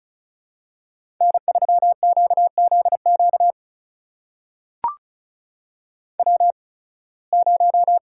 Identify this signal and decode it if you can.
Morse code, heard as a keyed tone, about 36 wpm, spaced as for 23 wpm, 700 Hz. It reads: N3QZQ W0
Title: USA Call Signs - 5 Characters - 8x Character Spacing - 35wpm